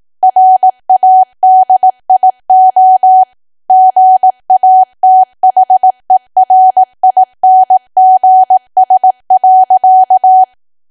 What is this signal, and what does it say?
Morse code, heard as a keyed tone, about 18 wpm, 750 Hz. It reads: RADIO GATHERINGS.
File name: Archive / 18 WPM / 18WPM